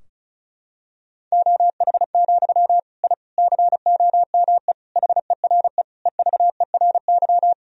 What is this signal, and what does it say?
Morse code, heard as a keyed tone, about 34 wpm, 700 Hz. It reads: OH, I COME HERE EVERY
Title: Sentences from Top 1000 Words - Review Part 2 - 35wpm